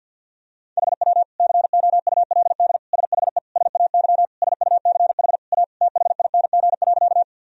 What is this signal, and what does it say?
Morse code, heard as a keyed tone, about 49 wpm, 700 Hz. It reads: HW COULD SHE SAY SUCH A THING.
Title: Sentences from Top 200 Words and Abbreviations 50wpm